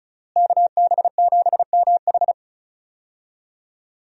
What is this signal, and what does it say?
Morse code, heard as a keyed tone, about 36 wpm, 700 Hz. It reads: KB7MH
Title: USA Call Signs - 5 Characters - Repeated 3x - 35wpm